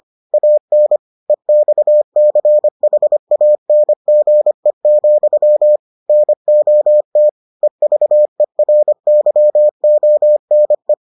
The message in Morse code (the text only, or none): AN EXCHANGE, NOT EVERYONE